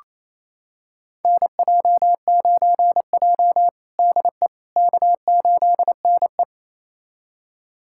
Morse code, heard as a keyed tone, about 28 wpm, 700 Hz. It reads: NJ9J DE K8NE
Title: USA Callsign DE Callsign - 4 Characters - 28wpm